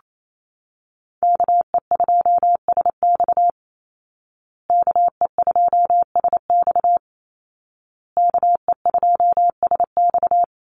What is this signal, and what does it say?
Morse code, heard as a keyed tone, about 28 wpm, 700 Hz. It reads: KE2SX KE2SX KE2SX
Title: Call Signs Repeated 2x - 28wpm